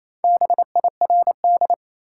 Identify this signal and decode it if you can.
Morse code, heard as a keyed tone, about 28 wpm, 700 Hz. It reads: BIRD
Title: Four Letter Words - 28wpm